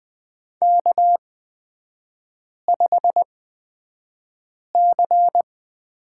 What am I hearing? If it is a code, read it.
Morse code, heard as a keyed tone, about 20 wpm, spaced as for 5 wpm, 700 Hz. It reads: K5C
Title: 229 Lesson 37e - Letter x - Three Characters 20wpm